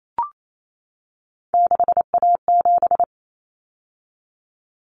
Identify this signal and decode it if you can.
Morse code, heard as a keyed tone, about 28 wpm, 700 Hz. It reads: 6A7
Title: Common Cut Numbers - 3-Digit 28wpm